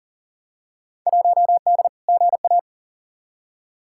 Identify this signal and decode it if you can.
Morse code, heard as a keyed tone, about 40 wpm, 700 Hz. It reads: 1D GA
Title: Field Day Exchanges - 40wpm